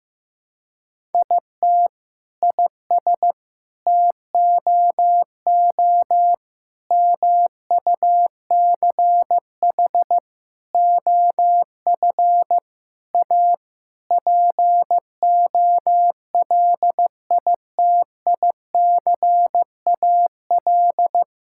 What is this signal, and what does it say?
Morse code, heard as a keyed tone, about 16 wpm, 700 Hz. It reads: IT IS TOO MUCH OF A POLITICAL
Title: Sentences from Top 1000 Words - Review Part 5 - 15wpm